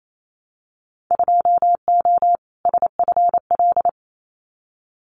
Morse code, heard as a keyed tone, about 28 wpm, 700 Hz. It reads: 2O SFL
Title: Winter Field Day Exchanges 28wpm